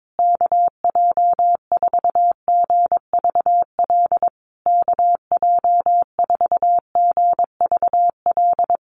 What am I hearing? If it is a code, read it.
Morse code, heard as a keyed tone, about 22 wpm, 700 Hz. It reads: KJ4GVL KJ4GVL
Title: USA Call Signs Repeated 3x 22wpm